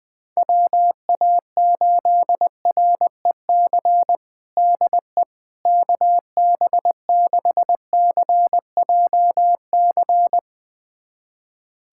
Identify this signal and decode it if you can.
Morse code, heard as a keyed tone, about 20 wpm, 700 Hz. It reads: WA8REC DE KB6CJC
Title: USA Callsign DE Callsign - 6 Characters - 20wpm